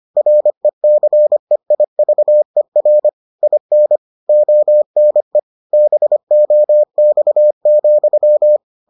Morse code, heard as a keyed tone, about 25 wpm, 600 Hz. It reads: RECEIVER IN ONE BOX,